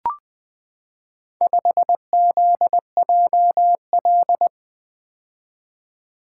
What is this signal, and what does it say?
Morse code, heard as a keyed tone, about 20 wpm, 700 Hz. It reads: 5ZJL
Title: Four Letter-Number Combinations - 20wpm